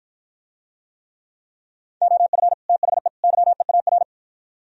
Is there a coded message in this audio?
Yes: Morse code from 2.0 to 4.0 s, heard as a keyed tone, about 52 wpm, 700 Hz. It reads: OF THE YEAR